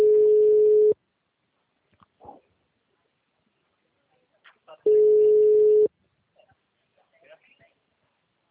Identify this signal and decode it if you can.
Ringback tone (Europe, 425 Hz).